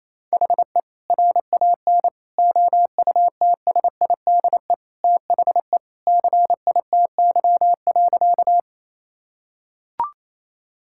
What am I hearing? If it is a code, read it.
Morse code, heard as a keyed tone, about 28 wpm, 700 Hz. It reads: HE RAN OUTSIDE THE CITY.
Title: Sentences from Top 1000 Words - Review Part 6 - 28wpm